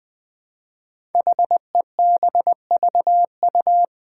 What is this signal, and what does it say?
Morse code, heard as a keyed tone, about 20 wpm, 700 Hz. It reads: HEBVU
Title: Five Challenging Letter-Number Combinations 20wpm